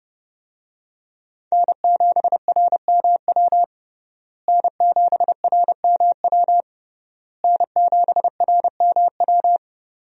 Morse code, heard as a keyed tone, about 30 wpm, 700 Hz. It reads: N7RMW N7RMW N7RMW
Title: USA Call Signs - 5 Characters - Repeated 3x - 30wpm